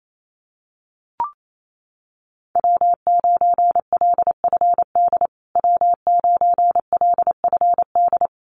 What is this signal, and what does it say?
Morse code, heard as a keyed tone, about 28 wpm, 700 Hz. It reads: W9LFD W9LFD
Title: USA Call Signs Repeated 2x 28wpm